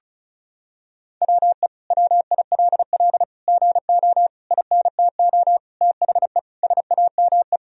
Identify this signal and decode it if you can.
Morse code, heard as a keyed tone, about 36 wpm, 700 Hz. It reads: WE WILL GO INTO THE SAME
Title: Sentences from Top 1000 Words - Review Part 4 - 35wpm